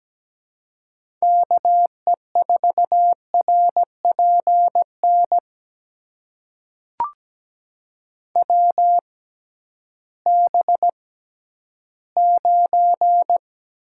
Morse code, heard as a keyed tone, about 17 wpm, spaced as for 11 wpm, 700 Hz. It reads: KE4RPN WB9
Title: USA Call Signs - 6 Characters - 6x Character Spacing - 17wpm